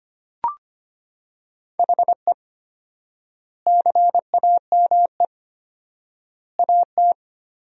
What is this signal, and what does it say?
Morse code, heard as a keyed tone, about 25 wpm, 700 Hz. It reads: HE CAME AT
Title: 4x Word-Spacing - Sentences from Top 300 Words - 25wpm